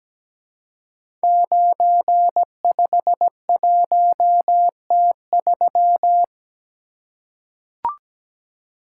Morse code, heard as a keyed tone, about 17 wpm, 700 Hz. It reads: 951T3
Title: Code Groups - 5 Letter:Numbers - 17wpm